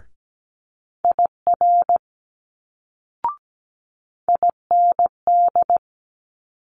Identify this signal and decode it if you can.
Morse code, heard as a keyed tone, about 17 wpm, 700 Hz. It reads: IR IND
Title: Multi-letter Phonemes Quick Fire 17wpm